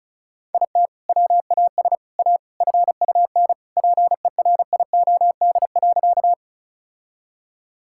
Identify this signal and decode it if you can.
Morse code, heard as a keyed tone, about 34 wpm, 700 Hz. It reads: IT WAS A FUN PERIOD.
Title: Sentences from Top 1000 Words - Review Part 5 - 35wpm